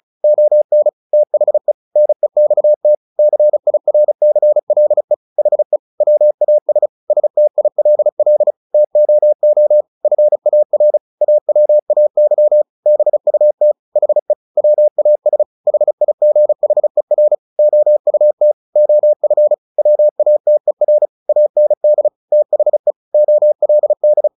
Morse code, heard as a keyed tone, about 34 wpm, 600 Hz. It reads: ON THE NEXT CIRCLE HE WAS STILL TOO FAR AWAY BUT HE WAS HIGHER OUT OF WATER AND THE OLD
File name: the old man and the sea 2 35wpm